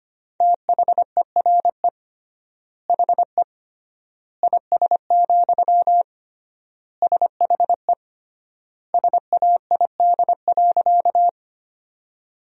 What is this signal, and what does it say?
Morse code, heard as a keyed tone, about 25 wpm, 700 Hz. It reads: THERE HE IS, SHE SAID.
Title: 3x Word-Spacing - Sentences from Top 100 Words - 25wpm